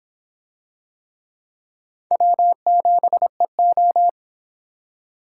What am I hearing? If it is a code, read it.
Morse code, heard as a keyed tone, about 26 wpm, 700 Hz. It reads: W7EO